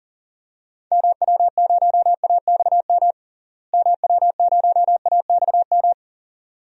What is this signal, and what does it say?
Morse code, heard as a keyed tone, about 40 wpm, 700 Hz. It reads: MW0AXM MW0AXM
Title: DXCC-100 Call Signs - 6 Characters - Repeated 2x - 40wpm